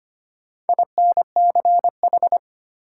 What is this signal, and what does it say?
Morse code, heard as a keyed tone, about 25 wpm, 700 Hz. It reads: INCH
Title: Sentences from Top 1000 Words - Review Part 3 - New Word Drill - 25wpm